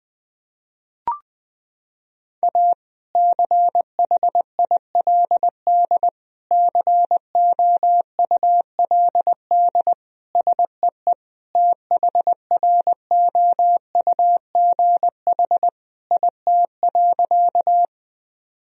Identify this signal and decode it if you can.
Morse code, heard as a keyed tone, about 21 wpm, 700 Hz. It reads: A CHILD COULD SEE THROUGH IT.
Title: Sentences from Top 1000 Words - Review Part 1 - 20wpm